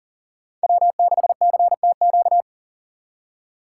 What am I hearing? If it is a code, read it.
Morse code, heard as a keyed tone, about 40 wpm, 690 Hz. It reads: W6CTQ
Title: USA Call Signs - 5 Characters - 6x Character Spacing - 40wpm